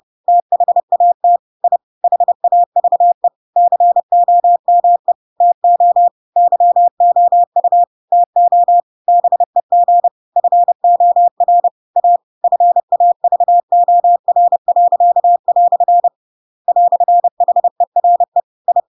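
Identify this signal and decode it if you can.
Morse code, heard as a keyed tone, about 30 wpm, 700 Hz. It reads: THAT I HAVE COME TO YOU TO BEG FOR A FAVOR." "HERE I
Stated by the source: Chapter 2 of The Adventures of Pinocchio read as a book in Morse code (CW) at 30 wpm, by ik1hge.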